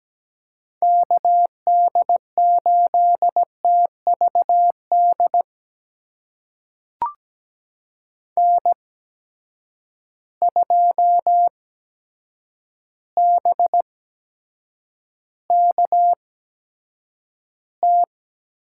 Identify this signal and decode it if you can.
Morse code, heard as a keyed tone, about 17 wpm, spaced as for 8 wpm, 700 Hz. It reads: KD8TVD N2BKT
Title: USA Call Signs - 8x Character Spacing - 17wpm